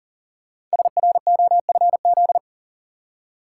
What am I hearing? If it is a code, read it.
Morse code, heard as a keyed tone, about 42 wpm, 700 Hz. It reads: SROFZ
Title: Code Groups - 5 Letters - 40wpm